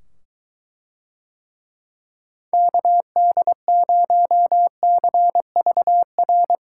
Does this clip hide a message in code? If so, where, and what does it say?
Yes: Morse code from 2.5 to 6.6 s, heard as a keyed tone, about 23 wpm, 700 Hz. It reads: KD0CVR